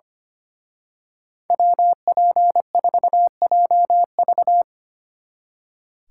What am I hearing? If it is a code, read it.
Morse code, heard as a keyed tone, about 28 wpm, 700 Hz. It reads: WP4JV
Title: Speed Racing - USA Call Signs 25wpm